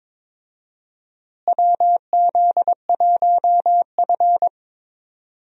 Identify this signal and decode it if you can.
Morse code, heard as a keyed tone, about 22 wpm, 700 Hz. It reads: WZ1F
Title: USA Call Signs - 4 Characters - Repeated 3x - 22wpm